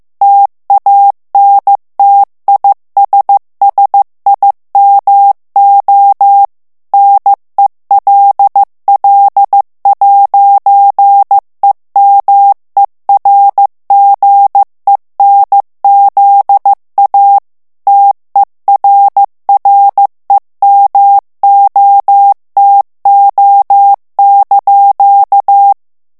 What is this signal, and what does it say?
Morse code, heard as a keyed tone, about 15 wpm, 800 Hz. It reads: TANTISSIMO NELL'EMERGENZA TERREMOTO)
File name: ITALIEN texte clair 1 vitesse 14 mots minute